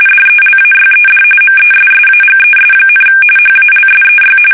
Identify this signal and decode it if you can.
RTTY, about 45.45 baud. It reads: DELAYED MESSAGES, HALSEY
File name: fsk850 / battleshipnj